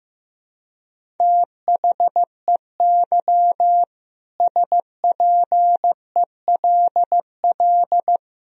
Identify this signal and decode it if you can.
Morse code, heard as a keyed tone, about 15 wpm, 700 Hz. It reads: THEY SPELL
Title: Sentences from Top 1000 Words - Review Part 1 - 15wpm